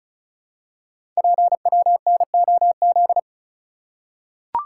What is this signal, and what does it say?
Morse code, heard as a keyed tone, about 34 wpm, 690 Hz. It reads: PWNOZ
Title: Code Groups - 5 Letters - 35wpm